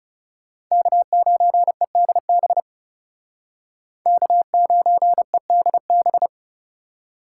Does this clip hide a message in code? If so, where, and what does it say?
Yes: Morse code from 0.7 to 6.3 s, heard as a keyed tone, about 38 wpm, 700 Hz. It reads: K9EDB K9EDB